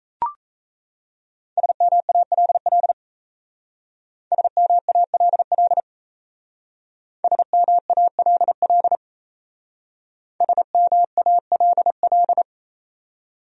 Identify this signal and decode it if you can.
Morse code, heard as a keyed tone, about 35 wpm, 700 Hz. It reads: SMALL SMALL SMALL SMALL